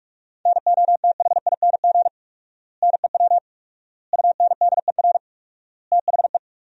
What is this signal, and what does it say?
Morse code, heard as a keyed tone, about 46 wpm, 700 Hz. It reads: NOTHING NEW UNDER THE
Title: 4x Word-Spacing - Sentences from Top 500 Words - 45wpm